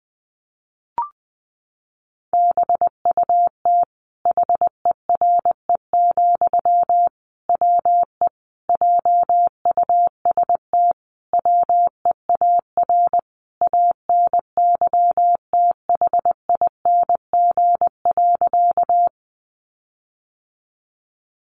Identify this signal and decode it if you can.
Morse code, heard as a keyed tone, about 20 wpm, 700 Hz. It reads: BUT HERE, WE JUST WEAR ANYTHING.
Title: Sentences from Top 1500 Words 20wpm